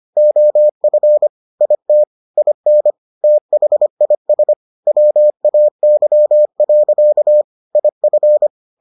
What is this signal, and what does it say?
Morse code, heard as a keyed tone, about 25 wpm, 600 Hz. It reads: OF IT IN THIS WAY. IF